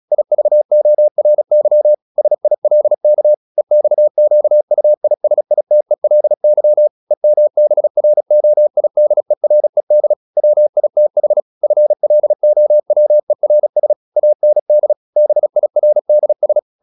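Morse code, heard as a keyed tone, about 36 wpm, 600 Hz. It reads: IVORY SILK EXQUISITELY EMBROIDERED WITH FLOWERS AND BIRDS
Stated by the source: Morse practice at 36 wpm, text of Ken Follett's novel Never, from CW med Gnister.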